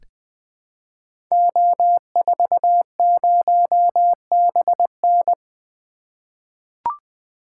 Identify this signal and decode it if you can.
Morse code, heard as a keyed tone, about 20 wpm, 700 Hz. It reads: O40BN
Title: Code Groups - 5 Letter:Numbers - 20wpm